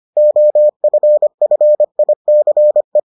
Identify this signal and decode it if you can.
Morse code, heard as a keyed tone, about 25 wpm, 600 Hz. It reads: OFFICE